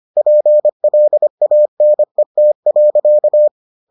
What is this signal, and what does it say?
Morse code, heard as a keyed tone, about 25 wpm, 600 Hz. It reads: PLANET.